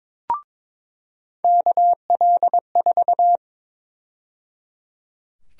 Morse code, heard as a keyed tone, about 22 wpm, 700 Hz. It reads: KL4
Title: USA Call Sign Prefixes - 22wpm